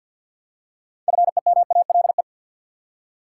Morse code, heard as a keyed tone, about 52 wpm, 700 Hz. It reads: FEMALE
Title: Six Letter Words - 50wpm